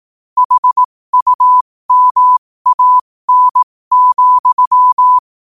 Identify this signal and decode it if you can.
Morse code, heard as a keyed tone, about 18 wpm, spaced as for 15 wpm, 1000 Hz. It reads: HUMAN,
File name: qotdfiles / 15wpm